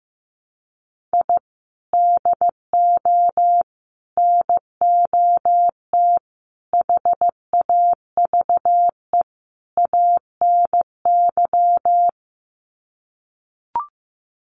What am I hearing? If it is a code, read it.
Morse code, heard as a keyed tone, about 15 wpm, 700 Hz. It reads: I DO NOT HAVE ANY
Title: Sets of 5 Words 15wpm